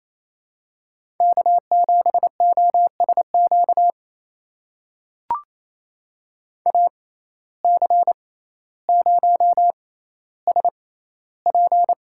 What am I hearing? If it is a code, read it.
Morse code, heard as a keyed tone, about 28 wpm, spaced as for 14 wpm, 700 Hz. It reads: K7OSQ AC0SP
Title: USA Call Signs - 5 Characters - 6x Character Spacing - 28wpm